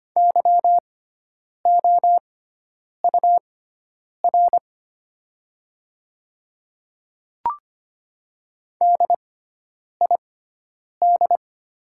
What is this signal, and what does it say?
Morse code, heard as a keyed tone, about 25 wpm, spaced as for 9 wpm, 700 Hz. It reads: YOUR DID
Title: 6x Character Spacing - Top 100 Words - 25wpm